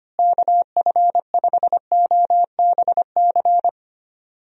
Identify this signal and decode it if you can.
Morse code, heard as a keyed tone, about 25 wpm, 700 Hz. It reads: KF5OBC
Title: USA Call Signs - 6 Characters - Repeated 2x - 25wpm